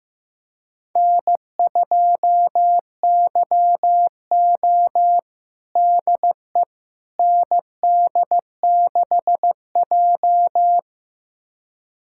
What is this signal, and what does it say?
Morse code, heard as a keyed tone, about 15 wpm, 700 Hz. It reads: N2YO DE ND6J
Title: USA Callsign DE Callsign - 4 Characters - 15wpm